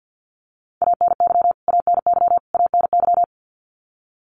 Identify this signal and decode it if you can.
Morse code, heard as a keyed tone, about 50 wpm, 700 Hz. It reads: ANY ANY ANY
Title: Top 300 Words Repeated 50wpm